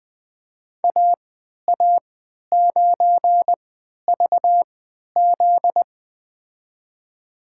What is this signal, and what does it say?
Morse code, heard as a keyed tone, about 20 wpm, spaced as for 11 wpm, 700 Hz. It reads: AA9VZ
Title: USA Call Signs - 3x Character Spacing - 20wpm